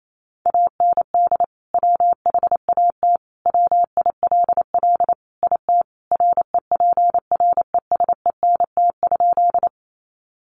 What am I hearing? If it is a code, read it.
Morse code, heard as a keyed tone, about 28 wpm, 700 Hz. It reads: AND WHAT WILL IT REPRESENT?
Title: Sentences from Top 1200 Words 28wpm